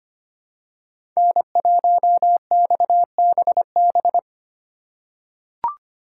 Morse code, heard as a keyed tone, about 25 wpm, 700 Hz. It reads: N1XBB
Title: USA Call Signs - 4x Character Spacing - 25wpm